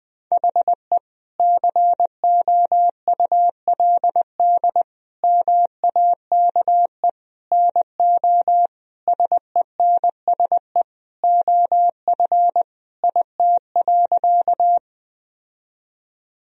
Morse code, heard as a keyed tone, about 20 wpm, 700 Hz. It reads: HE COULD MAKE NO SENSE OF IT.
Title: Sentences from Top 1000 Words - Review Part 4 - 20wpm